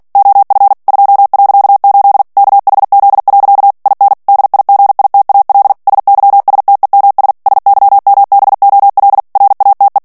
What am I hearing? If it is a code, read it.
Morse code, heard as a keyed tone, about 48 wpm, 800 Hz. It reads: OF 2.8 KHZ. IN DIGITAL SYSTEMS SYMBOL RATE
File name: ANGLAIS texte clair 3 vitesse 45 mots minute